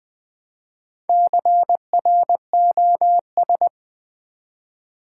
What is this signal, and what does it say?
Morse code, heard as a keyed tone, about 20 wpm, 700 Hz. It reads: CROS
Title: Build A Word - 1 Letter At A Time - 5-Letter Words - 20wpm